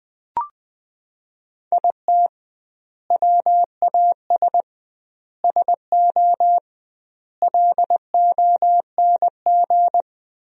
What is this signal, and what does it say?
Morse code, heard as a keyed tone, about 20 wpm, 700 Hz. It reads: IT WAS SO LONG
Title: 2x Word-Spacing - Sentences from Top 500 Words - 20wpm